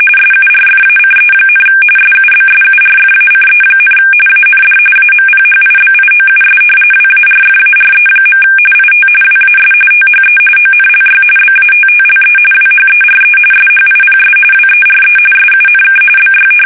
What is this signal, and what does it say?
RTTY, about 45.45 baud. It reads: SURFACE, SUBSURFACE, AND AMPHIBIOUS WARFARE. IT WAS THE ONLY BATTLE WHERE SHIPS WERE SUNK BY
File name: fsk850 / battleshipnj